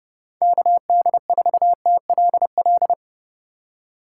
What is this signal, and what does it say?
Morse code, heard as a keyed tone, about 32 wpm, 700 Hz. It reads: KD4TLL
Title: USA Call Signs - 3x Character Spacing - 30wpm